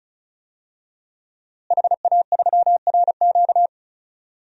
Morse code, heard as a keyed tone, about 36 wpm, 700 Hz. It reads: HA3RQ
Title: DXCC-100 Call Signs - 5 Characters - Repeated 2x - 35wpm